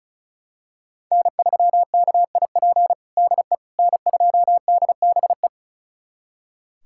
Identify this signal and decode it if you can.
Morse code, heard as a keyed tone, about 36 wpm, 700 Hz. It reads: N3KIP DE N2DBE
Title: USA Callsign DE Callsign - 5 Characters - 35wpm